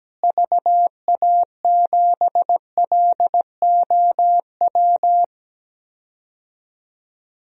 Morse code, heard as a keyed tone, about 17 wpm, 700 Hz. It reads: VA7LOW
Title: DXCC-100 Call Signs - 6 Characters - Repeated 3x - 17wpm